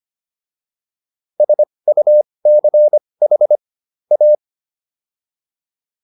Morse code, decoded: SUCH A

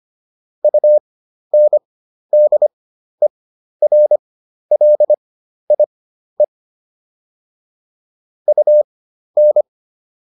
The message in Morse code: UNDERLIE UN

